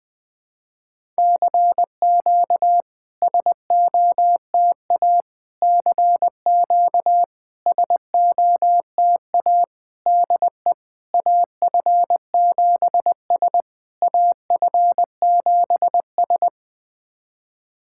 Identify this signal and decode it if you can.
Morse code, heard as a keyed tone, about 20 wpm, 700 Hz. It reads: CQ SOTA CQ SOTA DE AF7S AF7S
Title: SOTA QSO Practice 20wpm